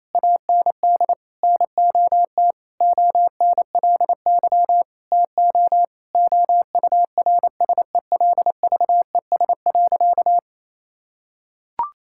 Morse code, heard as a keyed tone, about 28 wpm, 700 Hz. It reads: AND NOT ONLY TO OURSELVES.
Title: Sentences from Top 1600 Words 28wpm